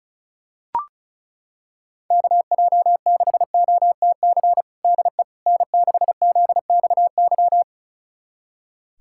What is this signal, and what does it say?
Morse code, heard as a keyed tone, about 34 wpm, 700 Hz. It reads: KJ6OTC DE N6ZXY
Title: USA Callsign DE Callsign 35wpm